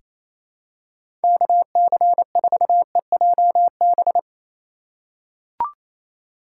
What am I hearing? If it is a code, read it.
Morse code, heard as a keyed tone, about 28 wpm, 700 Hz. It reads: KC4EJB
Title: USA Call Signs - 6 Characters - 3x Character Spacing - 28wpm